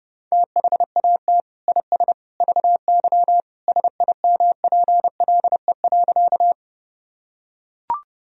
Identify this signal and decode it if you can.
Morse code, heard as a keyed tone, about 28 wpm, 700 Hz. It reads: THAT IS VY SIMPLE.
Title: Sentences from Top 400 Words and Abbreviations 30wpm